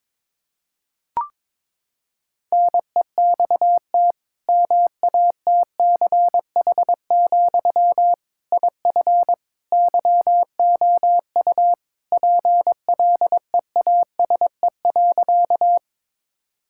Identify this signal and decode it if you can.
Morse code, heard as a keyed tone, about 22 wpm, 700 Hz. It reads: NEXT MATCH, IF YOU PLEASE.
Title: Sentences from Top 1600 Words 22wpm